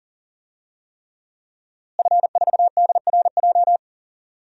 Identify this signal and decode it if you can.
Morse code, heard as a keyed tone, about 37 wpm, 700 Hz. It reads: F4DRJ
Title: DXCC-100 Call Signs - 5 Characters - Repeated 3x - 40wpm